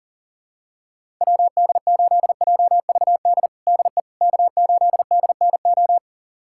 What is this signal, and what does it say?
Morse code, heard as a keyed tone, about 40 wpm, 700 Hz. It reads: WD8JVD DE K8DNO
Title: USA Callsign DE Callsign 40wpm